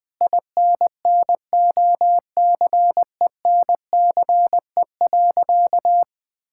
Morse code, heard as a keyed tone, about 20 wpm, 700 Hz. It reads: INNOCENCE.
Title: Sentences with 7-Word Phrases - Spoken Beginning - 20wpm